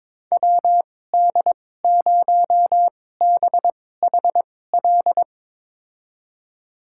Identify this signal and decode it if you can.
Morse code, heard as a keyed tone, about 22 wpm, spaced as for 16 wpm, 700 Hz. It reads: WD0BHL
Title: USA Call Signs - 6 Characters - 2x Character Spacing - 22wpm